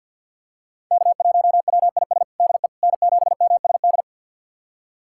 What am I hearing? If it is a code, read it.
Morse code, heard as a keyed tone, about 45 wpm, 700 Hz. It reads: K1WIS DE N7MSD
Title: USA Callsign DE Callsign 50wpm